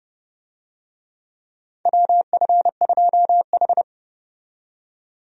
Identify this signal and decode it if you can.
Morse code, heard as a keyed tone, about 30 wpm, 700 Hz. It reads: WF2H
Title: USA Call Signs - 4 Characters - 30wpm